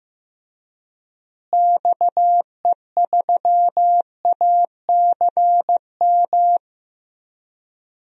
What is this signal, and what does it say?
Morse code, heard as a keyed tone, about 16 wpm, 700 Hz. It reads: XE3ACM
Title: DXCC-100 Call Signs - 6 Characters - Repeated 3x - 15wpm